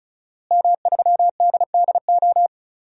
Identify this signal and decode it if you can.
Morse code, heard as a keyed tone, about 34 wpm, 700 Hz. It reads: M3DDO
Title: DXCC-100 Call Signs - 5 Characters - Repeated 3x - 35wpm